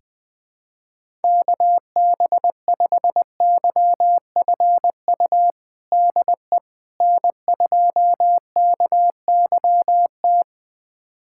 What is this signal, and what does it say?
Morse code, heard as a keyed tone, about 20 wpm, 700 Hz. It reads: KB5YFU DE N2KYT
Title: USA Callsign DE Callsign 20wpm